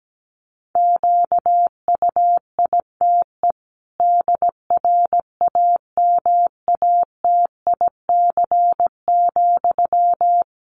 Morse code, heard as a keyed tone, about 17 wpm, 700 Hz. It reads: QUITE DRAMATIC,
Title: Sentences from Top 1300 Words 17wpm